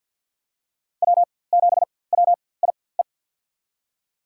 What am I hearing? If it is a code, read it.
Morse code, heard as a keyed tone, about 51 wpm, spaced as for 23 wpm, 690 Hz. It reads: W7WIE
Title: USA Call Signs - 4x Character Spacing - 50wpm